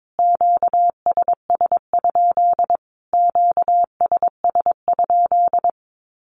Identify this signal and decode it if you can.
Morse code, heard as a keyed tone, about 22 wpm, 700 Hz. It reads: QSS? QSS?
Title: Q-Codes Repeated 3x 22wpm